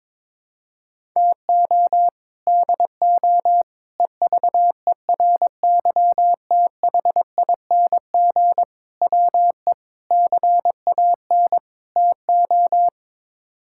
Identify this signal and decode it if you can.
Morse code, heard as a keyed tone, about 22 wpm, 700 Hz. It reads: TO DO EVERYTHING WE CAN TO
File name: practice / 22wpm / Sets of 6 Words - 22wpm